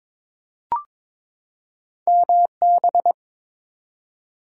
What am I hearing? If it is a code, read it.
Morse code, heard as a keyed tone, about 22 wpm, 700 Hz. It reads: MB
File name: ARRL Sections v2 - 22wpm